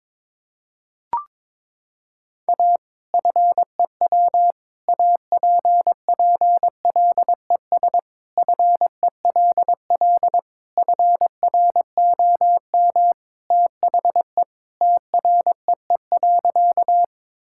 Morse code, heard as a keyed tone, about 22 wpm, 700 Hz. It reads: A FEW APPLES FELL FROM THE TREE.
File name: practice / 22wpm / Sentences with 2-Word Phrases - 22wpm